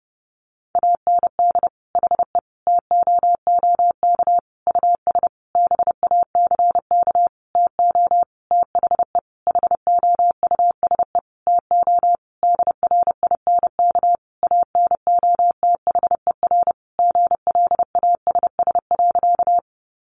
Morse code, heard as a keyed tone, about 30 wpm, 700 Hz. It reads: AND HE TOOK US BACK TO THE HOUSE TO DRINK ANOTHER GLASS.
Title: Sentences from Top 1500 Words 30wpm